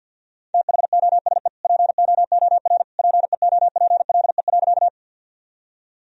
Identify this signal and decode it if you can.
Morse code, heard as a keyed tone, about 47 wpm, 700 Hz. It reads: THOSE POOR PEOPLE.
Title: Sentences from Top 1000 Words - Review Part 1 - 50wpm